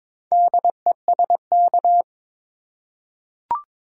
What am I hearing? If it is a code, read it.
Morse code, heard as a keyed tone, about 22 wpm, 700 Hz. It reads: DESK
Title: Four Letter Words - 22wpm